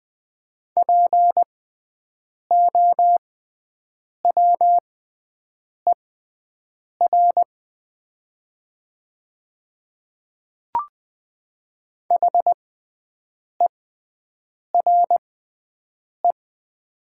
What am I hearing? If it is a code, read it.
Morse code, heard as a keyed tone, about 20 wpm, spaced as for 7 wpm, 700 Hz. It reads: POWER HERE
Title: 6x Character Spacing - Top 500 Words - 20wpm